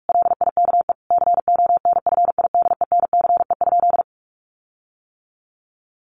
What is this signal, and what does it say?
Morse code, heard as a keyed tone, about 44 wpm, 700 Hz. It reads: LIKE CONFIDENCE?
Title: Sentences from Top 1500 Words 45wpm